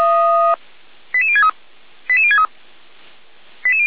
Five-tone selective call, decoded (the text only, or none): 88063 88063